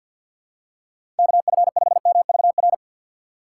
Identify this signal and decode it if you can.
Morse code, heard as a keyed tone, about 52 wpm, 700 Hz. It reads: KF5MVR
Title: USA Call Signs - 6 Characters - 4x Character Spacing - 50wpm